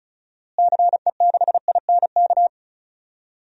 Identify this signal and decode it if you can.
Morse code, heard as a keyed tone, about 36 wpm, 700 Hz. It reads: CE6INK
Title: DXCC-100 Call Signs - 6 Characters - Repeated 2x - 35wpm